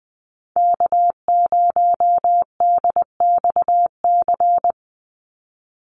Morse code, heard as a keyed tone, about 20 wpm, 700 Hz. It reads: K0DXC
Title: Call Signs Repeated 2x - 20wpm